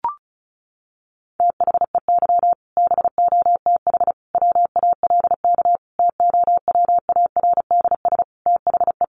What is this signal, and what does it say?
Morse code, heard as a keyed tone, about 34 wpm, 700 Hz. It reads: THEY BOTH WALK TOWARDS THE
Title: Additional Easy Phrases 35wpm